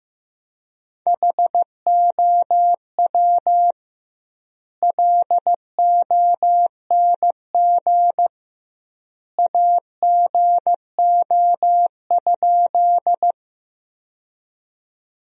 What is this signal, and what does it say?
Morse code, heard as a keyed tone, about 15 wpm, 700 Hz. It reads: HOW LONG AGO?
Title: 2x Word-Spacing - Sentences from Top 500 Words - 15wpm